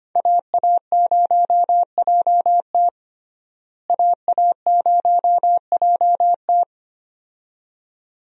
Morse code, heard as a keyed tone, about 25 wpm, 700 Hz. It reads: AA0JT AA0JT
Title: USA Call Signs - 5 Characters - Repeated 2x - 25wpm